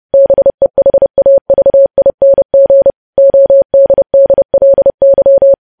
Morse code, heard as a keyed tone, about 30 wpm, 550 Hz. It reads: BEHAVING ODDLY
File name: podcast / News headlines-1x-30x30